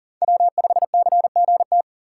Morse code, heard as a keyed tone, about 40 wpm, 700 Hz. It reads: W5CGT
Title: USA Call Signs - 5 Characters - 6x Character Spacing - 40wpm